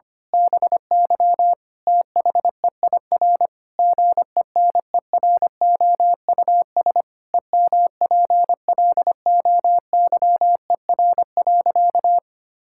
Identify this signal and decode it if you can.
Morse code, heard as a keyed tone, about 25 wpm, 700 Hz. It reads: BY THEIR GENEROUS EMPLOYER.
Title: Pangram Challenge v2 - 25wpm